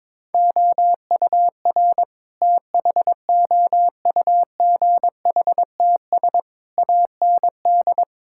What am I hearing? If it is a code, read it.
Morse code, heard as a keyed tone, about 22 wpm, 700 Hz. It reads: OUR THOUGHTS AND
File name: practice / 22wpm / Sets of 8 Words - 22wpm